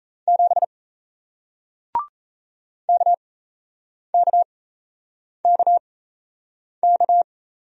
Morse code, heard as a keyed tone, about 38 wpm, spaced as for 8 wpm, 700 Hz. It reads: 7 KKKK